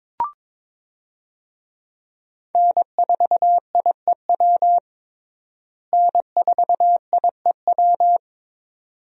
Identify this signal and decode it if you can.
Morse code, heard as a keyed tone, about 22 wpm, 700 Hz. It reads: N4IEW N4IEW
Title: USA Call Signs - 5 Characters - Repeated 2x - 22wpm